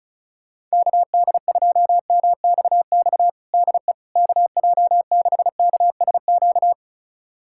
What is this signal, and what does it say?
Morse code, heard as a keyed tone, about 34 wpm, 700 Hz. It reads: KD2MXX DE KJ6KSQ
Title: USA Callsign DE Callsign 35wpm